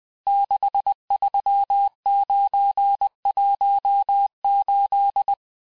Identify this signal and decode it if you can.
Morse code, heard as a keyed tone, about 20 wpm, 800 Hz. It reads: 63918
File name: cours-858-texte-chiffre-vitesse-19mpm